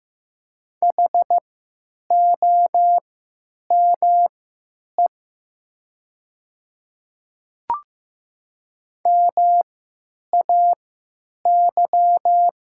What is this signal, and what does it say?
Morse code, heard as a keyed tone, about 15 wpm, spaced as for 9 wpm, 700 Hz. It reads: HOME MAY
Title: 3x Character Spacing - Top 500 Words - 15wpm